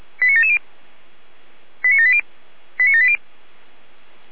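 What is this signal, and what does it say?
Five-tone selective call (ZVEI), heard as quick runs of five tones, tones of about 70 ms. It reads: 89770 89780 89780
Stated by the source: ZVEI five-tone selective call received on SDR, from feuerwehr-orlamuende.org.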